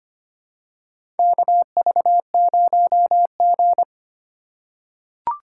Morse code, heard as a keyed tone, about 25 wpm, 700 Hz. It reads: KV0G
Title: USA Call Signs - 4 Characters - 4x Character Spacing - 25wpm